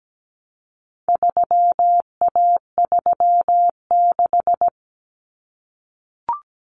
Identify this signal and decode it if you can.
Morse code, heard as a keyed tone, about 17 wpm, 700 Hz. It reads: 3A36
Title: Common Cut Numbers - 4-Digit 17wpm